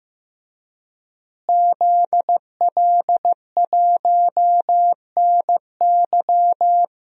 Morse code, heard as a keyed tone, about 15 wpm, 700 Hz. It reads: ZL1NY
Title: DXCC-100 Call Signs - 5 Characters - Repeated 3x - 15wpm